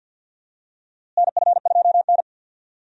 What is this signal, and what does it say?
Morse code, heard as a keyed tone, about 50 wpm, 700 Hz. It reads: NF2N
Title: USA Call Signs - 4 Characters - Repeated 3x - 50wpm